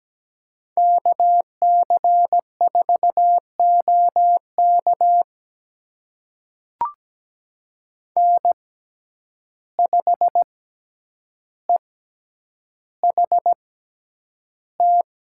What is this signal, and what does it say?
Morse code, heard as a keyed tone, about 17 wpm, spaced as for 9 wpm, 700 Hz. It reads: KC4OK N5EHT
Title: USA Call Signs - 6x Character Spacing - 17wpm